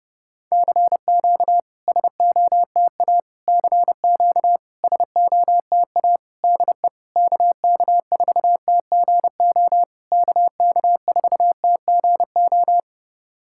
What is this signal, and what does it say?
Morse code, heard as a keyed tone, about 32 wpm, 690 Hz. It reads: CQ SOTA CQ SOTA DE KK4TGO KK4TGO
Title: SOTA QSO Practice 30wpm